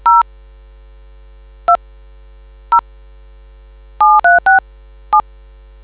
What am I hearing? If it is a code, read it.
DTMF, heard as telephone touch tones, tones of about 140 ms. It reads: 0 2 0 7367